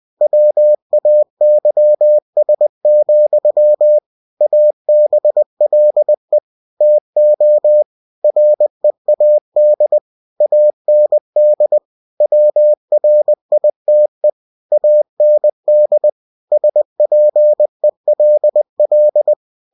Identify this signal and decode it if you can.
Morse code, heard as a keyed tone, about 20 wpm, 600 Hz. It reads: WAYS, ABLE TO READ AND WRITE AND SPELL
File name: cw-books / winnie-the-pooh / winnie-20wpm-page-0029